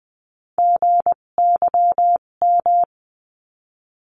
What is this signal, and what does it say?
Morse code, heard as a keyed tone, about 20 wpm, spaced as for 17 wpm, 700 Hz. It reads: GYM
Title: 257 Lesson 42b - Words 14-20wpm